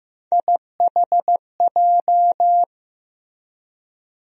Morse code, heard as a keyed tone, about 15 wpm, 700 Hz. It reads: IHJ
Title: USA Call Sign Postfixes - 3 Letters 15wpm